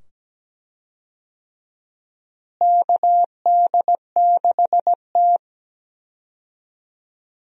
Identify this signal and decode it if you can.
Morse code, heard as a keyed tone, about 17 wpm, 700 Hz. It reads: KD6T
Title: USA Call Signs - 4 Characters - Repeated 2x - 17wpm